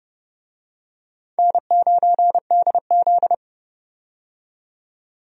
Morse code, heard as a keyed tone, about 32 wpm, 700 Hz. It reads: N9DZ
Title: USA Call Signs - 4 Characters - 30wpm